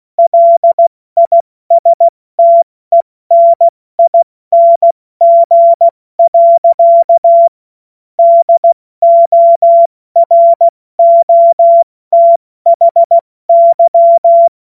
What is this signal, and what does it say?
Morse code, heard as a keyed tone, about 16 wpm, 680 Hz. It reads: LISTENING. DOROTHY
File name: Morsekurs / Witze / Witz 27. September 2022 70 F80